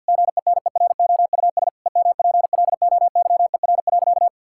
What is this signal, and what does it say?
Morse code, heard as a keyed tone, about 50 wpm, 700 Hz. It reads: GENEROUS EMPLOYER.
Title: Pangram Challenge v2 - 50wpm